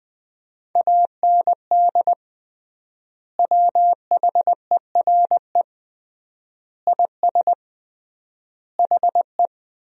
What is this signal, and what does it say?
Morse code, heard as a keyed tone, about 20 wpm, 700 Hz. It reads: AND WHERE IS HE
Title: 3x Word-Spacing - Sentences from Top 200 Words - 20wpm